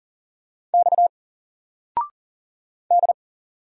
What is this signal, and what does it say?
Morse code, heard as a keyed tone, about 40 wpm, 700 Hz. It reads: X D
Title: Single Letter-Number 40wpm